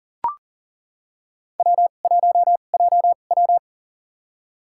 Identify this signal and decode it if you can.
Morse code, heard as a keyed tone, about 40 wpm, spaced as for 29 wpm, 700 Hz. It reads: W1JW
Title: USA Call Signs - 2x Character Spacing - 40wpm